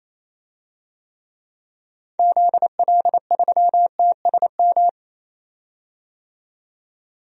Morse code, heard as a keyed tone, about 28 wpm, 700 Hz. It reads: ZL3TSM
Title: DXCC-100 Call Signs - 6 Characters - 28wpm